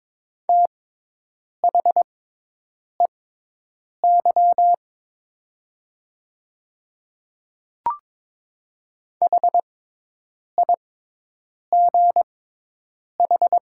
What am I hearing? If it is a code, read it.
Morse code, heard as a keyed tone, about 22 wpm, spaced as for 8 wpm, 700 Hz. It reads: THEY HIGH
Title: 6x Character Spacing - Top 200 Words - 22wpm